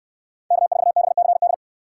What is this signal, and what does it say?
Morse code, heard as a keyed tone, about 68 wpm, 700 Hz. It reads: K4BCD